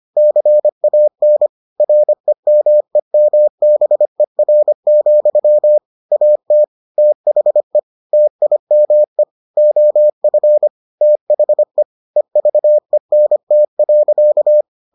Morse code, heard as a keyed tone, about 25 wpm, 600 Hz. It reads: CAN REMEMBER, AT THE TIME OF THE EVENT.